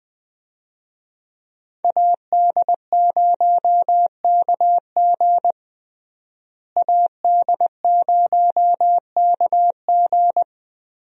Morse code, heard as a keyed tone, about 20 wpm, 700 Hz. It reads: AD0KG AD0KG
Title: USA Call Signs - 5 Characters - Repeated 3x - 20wpm